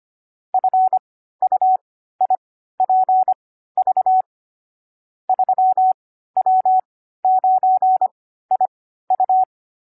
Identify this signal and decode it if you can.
Morse code, heard as a keyed tone, about 25 wpm, spaced as for 14 wpm, 750 Hz. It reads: FUIPV 3W9IU
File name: CW-Koch / koch.29.3.groups5.15wpm